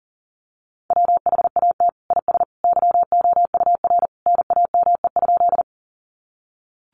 Morse code, heard as a keyed tone, about 37 wpm, 700 Hz. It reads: WHAT IS YOUR NAME?
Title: Sentences from Top 200 Words 40wpm